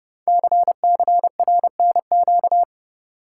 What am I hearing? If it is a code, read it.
Morse code, heard as a keyed tone, about 30 wpm, 700 Hz. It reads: CCRNQ
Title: Code Groups - 5 Letter:Numbers - 30wpm